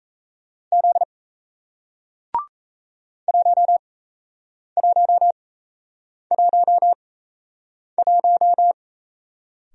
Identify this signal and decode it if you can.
Morse code, heard as a keyed tone, about 38 wpm, spaced as for 8 wpm, 700 Hz. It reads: Z 1111